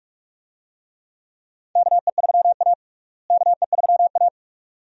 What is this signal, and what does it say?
Morse code, heard as a keyed tone, about 46 wpm, 700 Hz. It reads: KE3A KE3A
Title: USA Call Signs - 4 Characters - Repeated 2x - 45wpm